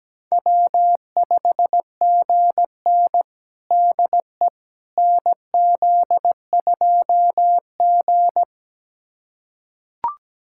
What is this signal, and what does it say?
Morse code, heard as a keyed tone, about 17 wpm, 700 Hz. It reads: W5GN DE NZ2G
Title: USA Callsign DE Callsign - 4 Characters - 17wpm